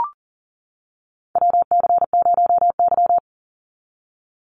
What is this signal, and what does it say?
Morse code, heard as a keyed tone, about 40 wpm, 700 Hz. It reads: WC0Y
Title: NA SOTA Chaser Callsigns 40wpm